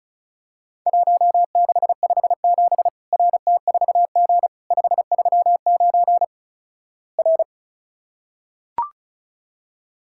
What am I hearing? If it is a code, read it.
Morse code, heard as a keyed tone, about 34 wpm, 700 Hz. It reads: 1657 RT4G 539 R
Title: Worked All Europe DX Contest - QTC Exchange - 35wpm